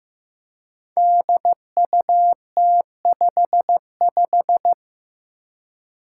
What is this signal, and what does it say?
Morse code, heard as a keyed tone, about 15 wpm, 700 Hz. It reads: DUT55
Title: Five Challenging Letter-Number Combinations 15wpm